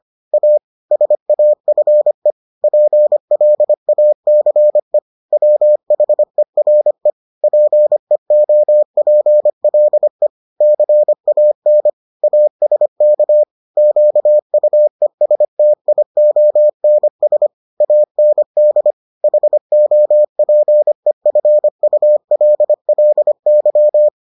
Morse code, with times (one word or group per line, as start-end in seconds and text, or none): 0.3-0.6 A
0.9-2.3 SAFE
2.6-5.0 PLACE
5.3-7.1 WHERE
7.4-10.3 PEOPLE
10.6-11.9 CAN
12.2-13.4 ASK
13.8-17.5 QUESTIONS
17.8-18.9 AND
19.2-24.1 HOPEFULLY